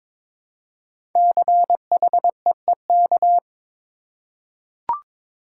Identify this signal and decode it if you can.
Morse code, heard as a keyed tone, about 22 wpm, 700 Hz. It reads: CHEEK
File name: 5 Letter Words - First 3 Letters Given - 22wpm